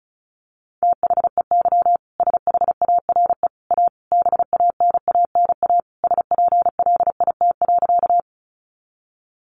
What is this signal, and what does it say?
Morse code, heard as a keyed tone, about 34 wpm, 700 Hz. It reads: THEY SHARE A BANANA SPLIT.
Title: Easy Phrases 35wpm